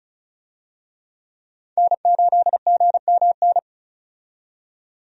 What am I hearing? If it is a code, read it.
Morse code, heard as a keyed tone, about 36 wpm, 700 Hz. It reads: N8GMN
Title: USA Call Signs - 5 Characters - 35wpm